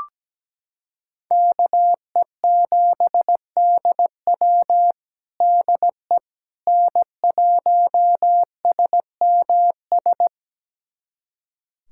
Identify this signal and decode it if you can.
Morse code, heard as a keyed tone, about 17 wpm, 700 Hz. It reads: KE7DW DE N1SMS
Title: USA Callsign DE Callsign - 5 Characters - 17wpm